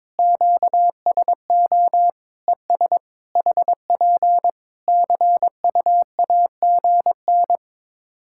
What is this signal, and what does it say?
Morse code, heard as a keyed tone, about 22 wpm, 700 Hz. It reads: QSO ES HP CUAGN
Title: Practice QSO - Exchange 4 - Ending - 22wpm